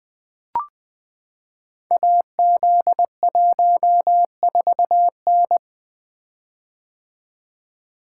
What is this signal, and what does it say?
Morse code, heard as a keyed tone, about 20 wpm, 700 Hz. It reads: AZ14N
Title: Code Groups - 5 Letter:Numbers - 20wpm